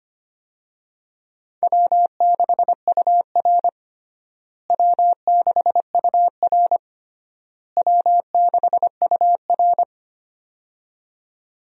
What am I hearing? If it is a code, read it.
Morse code, heard as a keyed tone, about 25 wpm, 700 Hz. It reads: W6UR W6UR W6UR
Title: USA Call Signs - 4 Characters - Repeated 3x - 25wpm